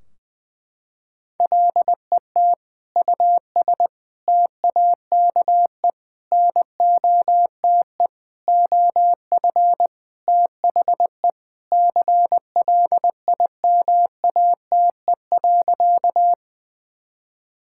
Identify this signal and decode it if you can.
Morse code, heard as a keyed tone, about 21 wpm, 700 Hz. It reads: LET US TAKE NOTE OF THE CLIMATE.
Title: Sentences from Top 1600 Words 20wpm